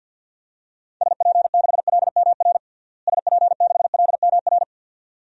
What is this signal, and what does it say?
Morse code, heard as a keyed tone, about 52 wpm, 700 Hz. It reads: SP6LMR SP6LMR
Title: DXCC-100 Call Signs - 6 Characters - Repeated 2x - 50wpm